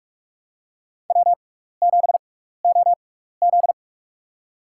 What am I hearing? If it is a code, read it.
Morse code, heard as a keyed tone, about 48 wpm, spaced as for 16 wpm, 700 Hz. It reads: W7OZ
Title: USA Call Signs - 4 Characters - 6x Character Spacing - 45wpm